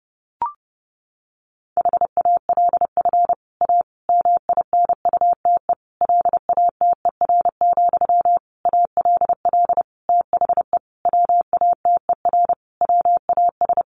Morse code, heard as a keyed tone, about 28 wpm, 700 Hz. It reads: HALF A MINUTE LATER, ALL THE WATER WAS
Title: Sentences from Top 1100 Words 30wpm